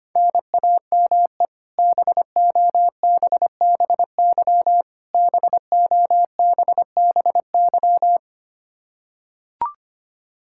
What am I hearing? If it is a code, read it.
Morse code, heard as a keyed tone, about 25 wpm, 700 Hz. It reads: NAME BOBBY BOBBY
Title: Practice QSO - Exchange 1 - Beginning - 25wpm